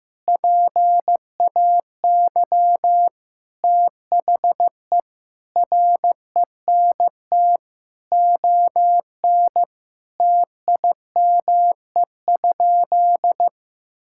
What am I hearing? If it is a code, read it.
Morse code, heard as a keyed tone, about 16 wpm, 700 Hz. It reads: PAY THE RENT ON TIME?
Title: Sentences from Top 1000 Words - Review Part 5 - 15wpm